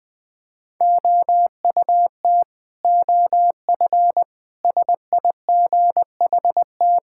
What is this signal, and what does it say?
Morse code, heard as a keyed tone, about 20 wpm, 700 Hz. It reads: OUT OF SIGHT
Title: Sentences with 7-Word Phrases - Spoken Beginning - 20wpm